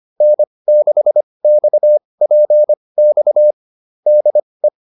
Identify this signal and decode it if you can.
Morse code, heard as a keyed tone, about 25 wpm, spaced as for 20 wpm, 600 Hz. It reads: N6XPX DE